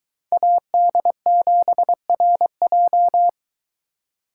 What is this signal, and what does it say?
Morse code, heard as a keyed tone, about 22 wpm, 700 Hz. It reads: AD7RJ